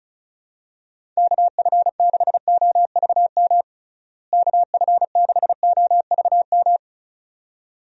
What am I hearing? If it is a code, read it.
Morse code, heard as a keyed tone, about 34 wpm, 700 Hz. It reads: KF6OVM KF6OVM
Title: USA Call Signs - 6 Characters - Repeated 2x - 35wpm